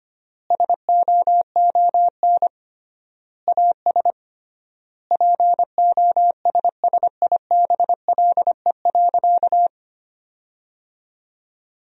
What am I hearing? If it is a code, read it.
Morse code, heard as a keyed tone, about 25 wpm, 700 Hz. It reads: SOON AS POSSIBLE.
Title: 3x Word-Spacing - Sentences from Top 500 Words - 25wpm